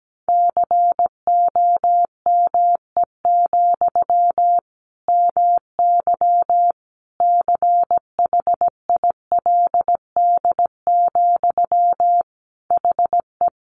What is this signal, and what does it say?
Morse code, heard as a keyed tone, about 17 wpm, 700 Hz. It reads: COME, MY CHILD, HE
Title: Sentences from Top 800 Words 17wpm